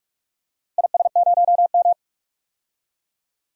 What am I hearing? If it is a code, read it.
Morse code, heard as a keyed tone, about 46 wpm, 700 Hz. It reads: IS0M
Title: DXCC-100 Call Signs - 4 Characters - 45wpm